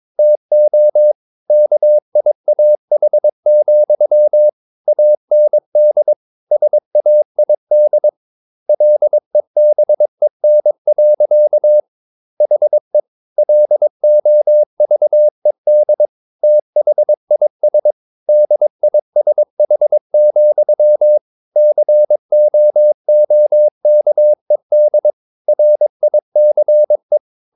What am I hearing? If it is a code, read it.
Morse code, heard as a keyed tone, about 22 wpm, 600 Hz. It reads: TO KIAH, AND SAID LEBEN. HE LOVED THIS DISH, COOKED RICE
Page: Romanen Never af Ken Follet på engelsk.